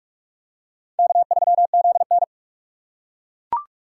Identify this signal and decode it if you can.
Morse code, heard as a keyed tone, about 40 wpm, 700 Hz. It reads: K3ZN